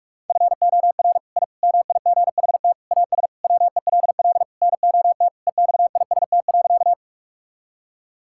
Morse code, heard as a keyed tone, about 41 wpm, 700 Hz. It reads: FOR I MIGHT AS WELL NOT EXIST.
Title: Sentences from Top 1000 Words - Review Part 5 - 45wpm